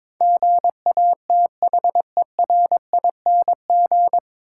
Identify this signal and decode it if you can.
Morse code, heard as a keyed tone, about 22 wpm, 700 Hz. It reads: GATHERING
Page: Morse Code Ninja practice 22wpm